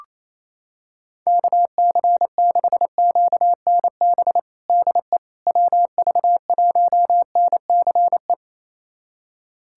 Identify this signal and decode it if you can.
Morse code, heard as a keyed tone, about 28 wpm, 700 Hz. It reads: KC6QNB DE WV1NCE
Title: USA Callsign DE Callsign - 6 Characters - 28wpm